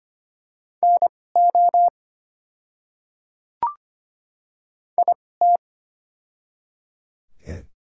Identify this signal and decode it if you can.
Morse code, heard as a keyed tone, about 25 wpm, spaced as for 18 wpm, 690 Hz. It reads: NO IT
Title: 2x Character Spacing - Top 200 Words - 25wpm